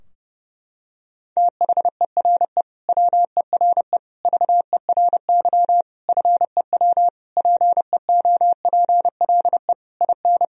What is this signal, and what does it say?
Morse code, heard as a keyed tone, about 30 wpm, 700 Hz. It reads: THERE WERE VERY FEW PEOPLE IN
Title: Sentences from Top 1000 Words - Review Part 2 - 30wpm